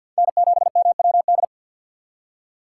Morse code, heard as a keyed tone, about 53 wpm, 700 Hz. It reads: N7MWD
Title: USA Call Signs - 5 Characters - 8x Character Spacing - 50wpm